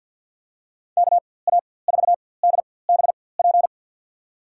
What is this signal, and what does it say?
Morse code, heard as a keyed tone, about 52 wpm, spaced as for 23 wpm, 700 Hz. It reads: KA4DBP
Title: USA Call Signs - 6 Characters - 4x Character Spacing - 50wpm